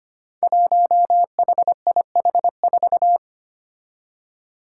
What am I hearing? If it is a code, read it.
Morse code, heard as a keyed tone, about 25 wpm, 700 Hz. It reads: 1HIH4
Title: Five Challenging Letter-Number Combinations 25wpm